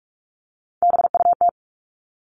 Morse code, heard as a keyed tone, about 44 wpm, 700 Hz. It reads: BUT
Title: Top 300 Words 45wpm